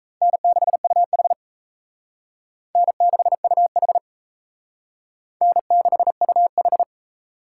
Morse code, heard as a keyed tone, about 34 wpm, 700 Hz. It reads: N6UH N6UH N6UH
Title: Speed Racing - USA 1x2 Call Signs 28wpm